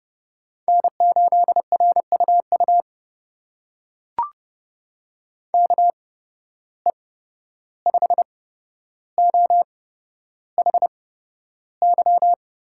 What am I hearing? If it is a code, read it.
Morse code, heard as a keyed tone, about 30 wpm, spaced as for 12 wpm, 700 Hz. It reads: N8RUU KE5OHY
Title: USA Call Signs - 8x Character Spacing - 30wpm